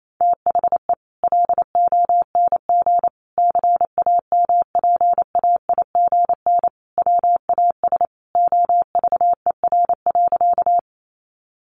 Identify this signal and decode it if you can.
Morse code, heard as a keyed tone, about 28 wpm, 700 Hz. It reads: THE LONG CAMPAIGN WAS OVER.
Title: Sentences from Top 1200 Words 28wpm